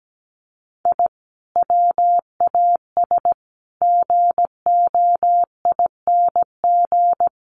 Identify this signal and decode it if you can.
Morse code, heard as a keyed tone, about 17 wpm, 700 Hz. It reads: I WAS GOING
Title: Sets of 5 Words 17wpm